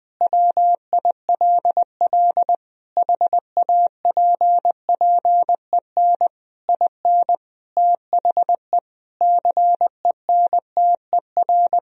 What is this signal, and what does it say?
Morse code, heard as a keyed tone, about 20 wpm, 700 Hz. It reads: WILL HAPPEN IN THE CENTER
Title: Sentences from Top 1000 Words - Review Part 2 - 20wpm